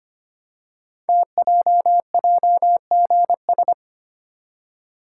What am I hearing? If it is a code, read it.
Morse code, heard as a keyed tone, about 25 wpm, 700 Hz. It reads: TJJGS
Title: Code Groups - 5 Letters - 25wpm